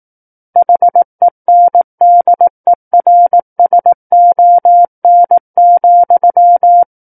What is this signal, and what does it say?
Morse code, heard as a keyed tone, about 18 wpm, 700 Hz. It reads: HENDERSON,
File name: 18 WPM